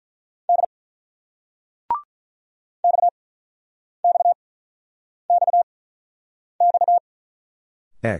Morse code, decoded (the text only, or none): D X X X X T